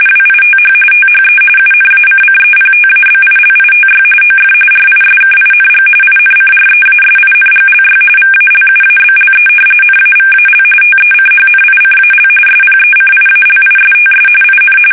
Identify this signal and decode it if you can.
RTTY, about 45.45 baud. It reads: WITH THE UNACCEPTABILITY OF RISKING INCREASINGLY VALUABLE NAVAL TECHNICAL SPECIALISTS IN
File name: fsk850